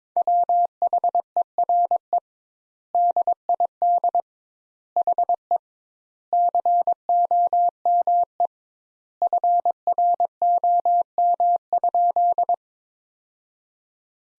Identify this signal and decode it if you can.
Morse code, heard as a keyed tone, about 22 wpm, 700 Hz. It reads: WHERE DID HE COME FROM?
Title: Sentences Top 200 Words - 22wpm